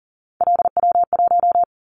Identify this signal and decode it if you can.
Morse code, heard as a keyed tone, about 40 wpm, 700 Hz. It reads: LW1
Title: Callsign Prefixes for Top 50 DXCC - 40wpm